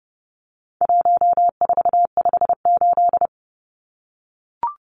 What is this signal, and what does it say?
Morse code, heard as a keyed tone, about 30 wpm, 700 Hz. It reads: 1458
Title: Numbers - 4-Digit 30wpm